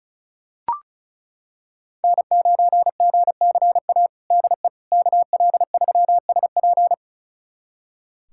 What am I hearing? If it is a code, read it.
Morse code, heard as a keyed tone, about 34 wpm, 700 Hz. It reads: N9GCA DE KL3SP
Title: USA Callsign DE Callsign - 5 Characters - 35wpm